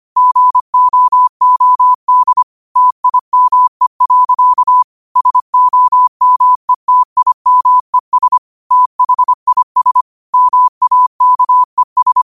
Morse code, heard as a keyed tone, about 25 wpm, 1000 Hz. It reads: GOOD TIME. SOMETIMES THIS MAKES
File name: qotdfiles / 25wpm